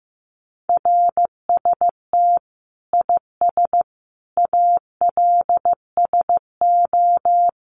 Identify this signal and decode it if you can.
Morse code, heard as a keyed tone, about 15 wpm, 700 Hz. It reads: RST IS ALSO
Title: Common QSO Phrases 15wpm